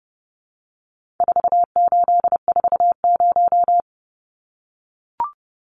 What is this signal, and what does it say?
Morse code, heard as a keyed tone, about 30 wpm, 700 Hz. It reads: 4840
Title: Numbers - 4-Digit 30wpm